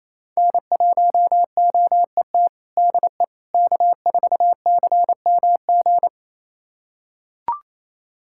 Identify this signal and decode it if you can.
Morse code, heard as a keyed tone, about 27 wpm, 700 Hz. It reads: N1OET DE K4CMG
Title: USA Callsign DE Callsign - 5 Characters - 28wpm